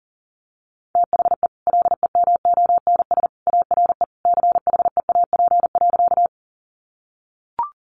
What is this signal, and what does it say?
Morse code, heard as a keyed tone, about 40 wpm, 700 Hz. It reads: THE LEMONS ARE CHEAP.
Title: Additional Easy Phrases 40wpm